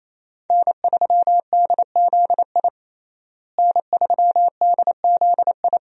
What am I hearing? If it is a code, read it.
Morse code, heard as a keyed tone, about 27 wpm, 700 Hz. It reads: N3DZI N3DZI
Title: USA Call Signs - 5 Characters - Repeated 2x - 28wpm